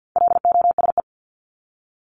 Morse code, heard as a keyed tone, about 50 wpm, 700 Hz. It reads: LOSE
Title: Top 1001-1250 Words 50wpm